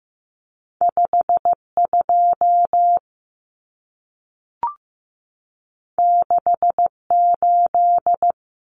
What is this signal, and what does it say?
Morse code, heard as a keyed tone, about 15 wpm, 700 Hz. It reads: 52 68
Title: Common Cut Numbers - 2-Digit 15wpm